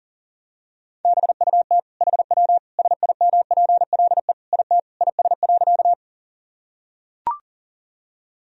Morse code, heard as a keyed tone, about 40 wpm, 700 Hz. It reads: BUT HW SIMPLE IT IS.
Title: Sentences from Top 400 Words and Abbreviations 40wpm